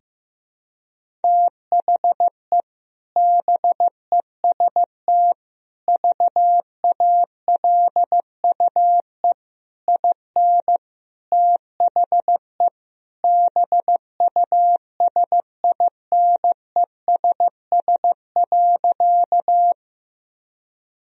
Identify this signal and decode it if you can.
Morse code, heard as a keyed tone, about 15 wpm, 700 Hz. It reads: THE BEST VALUE IN THE BUSINESS.
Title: Sentences from Top 1000 Words - Review Part 3 - 15wpm